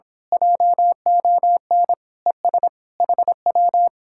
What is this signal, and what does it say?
Morse code, heard as a keyed tone, about 26 wpm, 700 Hz. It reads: JON ES HW
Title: Speed-Racing Common QSO Phrases - 22wpm